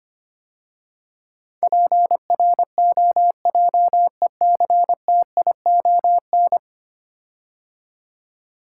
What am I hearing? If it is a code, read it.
Morse code, heard as a keyed tone, about 25 wpm, 700 Hz. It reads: PROJECTION
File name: 10 Letter Words - First 3 Letters Given - 25wpm